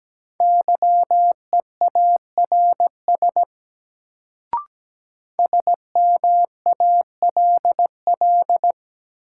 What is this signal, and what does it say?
Morse code, heard as a keyed tone, about 17 wpm, 700 Hz. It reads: YEARS SMALL
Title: Sentences from Top 200 Words - New Word Drill - 17wpm